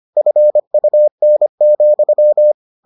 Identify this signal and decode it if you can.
Morse code, heard as a keyed tone, about 25 wpm, 600 Hz. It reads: FUN,